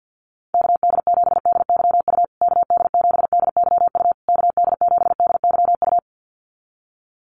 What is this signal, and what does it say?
Morse code, heard as a keyed tone, about 50 wpm, 700 Hz. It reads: KD7DYU KD7DYU KD7DYU
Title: USA Call Signs Repeated 3x 50wpm